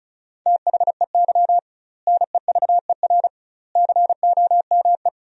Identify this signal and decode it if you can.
Morse code, heard as a keyed tone, about 34 wpm, 700 Hz. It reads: THEY NEVER COME
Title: 2x Word-Spacing - Sentences from Top 300 Words - 35wpm